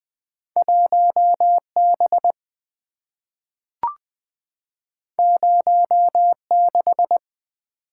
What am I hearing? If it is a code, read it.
Morse code, heard as a keyed tone, about 20 wpm, 700 Hz. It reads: 1 B 0 6
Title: Two Letter-Number Combinations - 20wpm